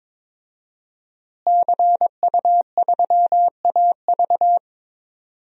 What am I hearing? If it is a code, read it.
Morse code, heard as a keyed tone, about 21 wpm, 700 Hz. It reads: CU3AV
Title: DXCC-100 Call Signs - 5 Characters - Repeated 2x - 22wpm